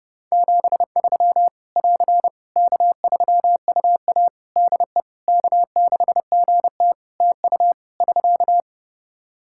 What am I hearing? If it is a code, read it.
Morse code, heard as a keyed tone, about 30 wpm, 700 Hz. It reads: 73 <AR> K3UA DE K6GT TU <SK>
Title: Practice QSO - Exchange 4 - Ending - 30wpm